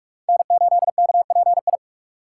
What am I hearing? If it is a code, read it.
Morse code, heard as a keyed tone, about 58 wpm, 700 Hz. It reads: N8KPI